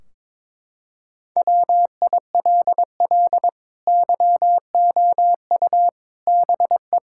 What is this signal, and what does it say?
Morse code, heard as a keyed tone, about 21 wpm, 700 Hz. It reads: WILL YOU BE
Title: Sentences from Top 1600 Words 22wpm